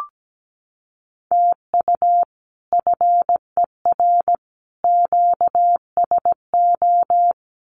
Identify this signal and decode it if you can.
Morse code, heard as a keyed tone, about 17 wpm, 700 Hz. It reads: TU FER QSO
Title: Common QSO Phrases 17wpm